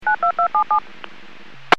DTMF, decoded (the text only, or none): 92277